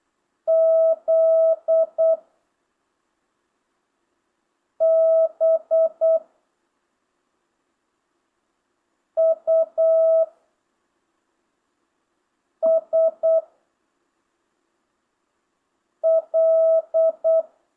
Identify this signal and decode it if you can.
Morse code, heard as a keyed tone, about 8 wpm, 660 Hz. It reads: Z B U S L
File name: jh8cbh / 2025cw